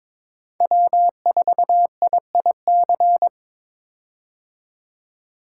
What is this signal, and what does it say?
Morse code, heard as a keyed tone, about 22 wpm, 700 Hz. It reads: W4IIC
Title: K1USN SST QSO Practice - 22wpm